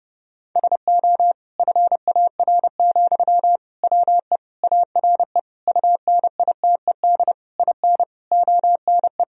Morse code, heard as a keyed tone, about 30 wpm, 700 Hz. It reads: SO FAR, WE ARE UNITED IN ONE
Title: Sentences from Top 1000 Words - Review Part 4 - 30wpm